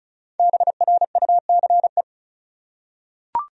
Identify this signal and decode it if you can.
Morse code, heard as a keyed tone, about 34 wpm, 700 Hz. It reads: BRUCE
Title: USA Common Names - Max 5 Letters - 35wpm